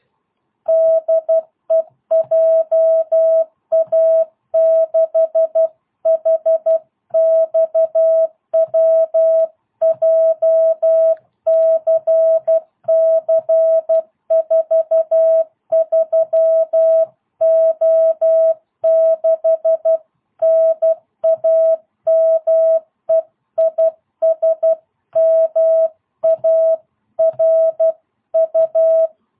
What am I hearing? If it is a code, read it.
Morse code, heard as a keyed tone, about 12 wpm, 660 Hz. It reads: DE JA6HXW JCC43O6 NAME IS MARU